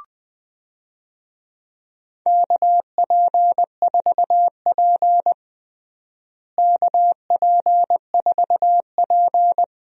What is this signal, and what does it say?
Morse code, heard as a keyed tone, about 20 wpm, 700 Hz. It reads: KP4P KP4P
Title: DXCC-100 Call Signs - 4 Characters - Repeated 2x - 20wpm